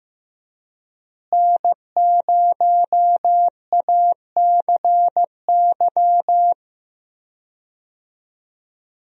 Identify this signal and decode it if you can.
Morse code, heard as a keyed tone, about 15 wpm, 700 Hz. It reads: N0ACY
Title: USA Call Signs - 5 Characters - Repeated 2x - 15wpm